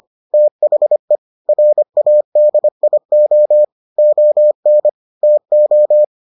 Morse code, heard as a keyed tone, about 25 wpm, 600 Hz. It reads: THE RADIO ON TO